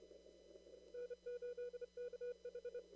Morse code, heard as a keyed tone, about 30 wpm, 490 Hz. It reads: N8KH
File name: CW / kt5j wpx cw-op-k5pi